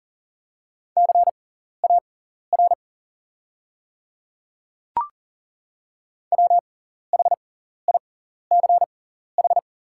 Morse code, heard as a keyed tone, about 40 wpm, spaced as for 14 wpm, 700 Hz. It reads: CAR WHICH
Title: 6x Character Spacing - Top 300 Words - 40wpm